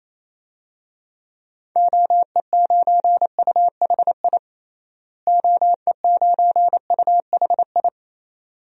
Morse code, heard as a keyed tone, about 28 wpm, 700 Hz. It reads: OE9UHI OE9UHI
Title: DXCC-100 Call Signs - 6 Characters - Repeated 2x - 28wpm